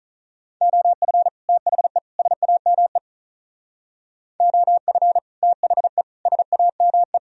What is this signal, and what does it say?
Morse code, heard as a keyed tone, about 43 wpm, 700 Hz. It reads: OF THE SAME OF THE SAME
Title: Speed-Racing - Sets of 3 Words 35wpm